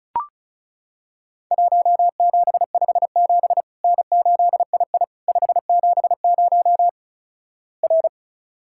Morse code, heard as a keyed tone, about 34 wpm, 700 Hz. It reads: 1757 N8II 570 R
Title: Worked All Europe DX Contest - QTC Exchange - 35wpm